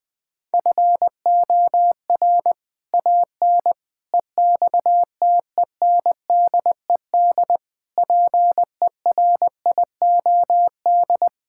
Morse code, heard as a keyed tone, about 20 wpm, 700 Hz. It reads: FOR AN EXTENDED PERIOD
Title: Sets of 6 Words - 20wpm